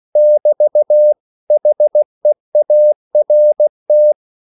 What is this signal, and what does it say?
Morse code, heard as a keyed tone, about 16 wpm, 600 Hz. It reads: <BT> HEART
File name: CW / Practice / 500MostCommonEnglishWords16